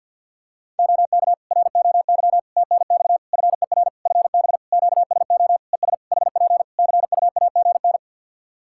Morse code, heard as a keyed tone, about 53 wpm, 700 Hz. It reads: OK ROY TNX FER FB QSO ES HP CUAGN
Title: Practice QSO - Exchange 4 - Ending - 50wpm